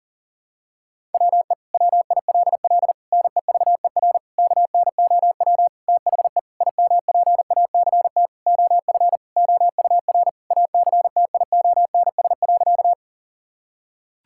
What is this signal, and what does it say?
Morse code, heard as a keyed tone, about 40 wpm, 700 Hz. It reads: WE WILL NEVER KNOW THE IMPACT OF OUR ACTIONS.
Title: Sentences from Top 1000 Words - Review Part 6 - 40wpm